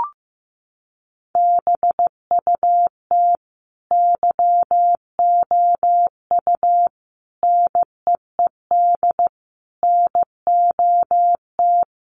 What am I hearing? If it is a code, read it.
Morse code, heard as a keyed tone, about 15 wpm, 700 Hz. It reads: BUT YOU NEED NOT
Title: Sentences from Top 1300 Words 15wpm